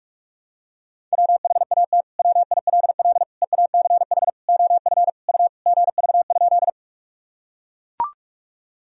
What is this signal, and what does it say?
Morse code, heard as a keyed tone, about 42 wpm, 700 Hz. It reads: WHAT WILL EACH OF U GV?
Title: Sentences from Top 200 Words and Abbreviations 45wpm